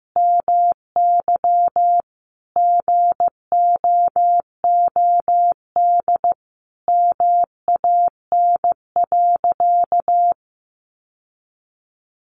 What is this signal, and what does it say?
Morse code, heard as a keyed tone, about 15 wpm, 700 Hz. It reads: MY GOOD MAN.
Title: Sentences from Top 1200 Words 15wpm